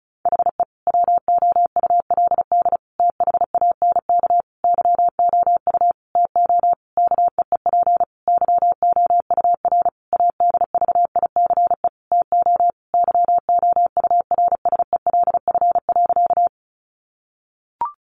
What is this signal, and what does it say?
Morse code, heard as a keyed tone, about 34 wpm, 700 Hz. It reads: HE WOULD THANK YOU TO KEEP YOUR ADVICE TO YOURSELF.
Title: Sentences from Top 1100 Words 35wpm